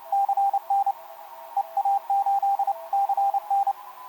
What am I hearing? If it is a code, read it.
Morse code, heard as a keyed tone, about 30 wpm, 810 Hz, with radio noise. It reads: CN EA8 CN
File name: kp2-ks9w sm1cq etc 12m cw 13mar2011 1339